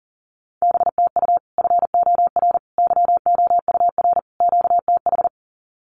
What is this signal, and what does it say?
Morse code, heard as a keyed tone, about 40 wpm, 700 Hz. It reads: BTU FOR YOUR QTH
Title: Common QSO Phrases 40wpm